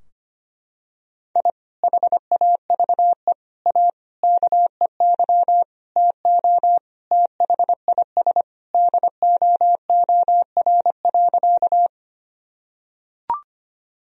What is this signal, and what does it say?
Morse code, heard as a keyed tone, about 25 wpm, 700 Hz. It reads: I HAVE A KEY TO THIS DOOR.
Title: Sentences from Top 1000 Words - Review Part 4 - 25wpm